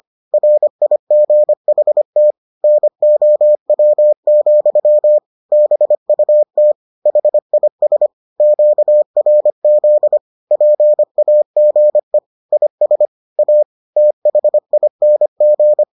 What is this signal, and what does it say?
Morse code, heard as a keyed tone, about 25 wpm, 600 Hz. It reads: RIGHT NOW, BUT HIS QRZ PAGE IS A THING